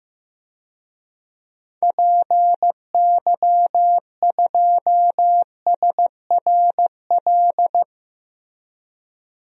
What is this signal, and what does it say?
Morse code, heard as a keyed tone, about 16 wpm, 700 Hz. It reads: PY2SRL
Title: DXCC-100 Call Signs - 6 Characters - Repeated 2x - 15wpm